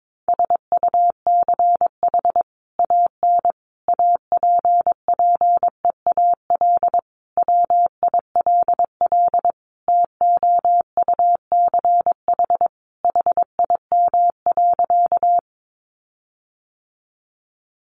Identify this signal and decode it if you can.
Morse code, heard as a keyed tone, about 22 wpm, 700 Hz. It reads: SUCH AN APPEAL WILL TOUCH HIM.
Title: Sentences from Top 1300 Words 22wpm